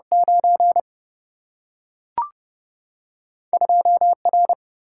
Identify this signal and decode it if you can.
Morse code, heard as a keyed tone, about 30 wpm, 700 Hz. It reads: 9 2 R
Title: Single-Letter Single-Number 30wpm